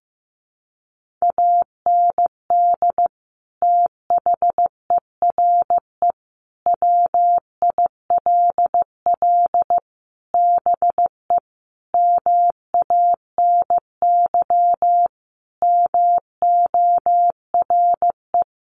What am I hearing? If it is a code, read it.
Morse code, heard as a keyed tone, about 14 wpm, 700 Hz. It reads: AND THERE WILL BE MANY MORE
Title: Sentences from Top 1500 Words 15wpm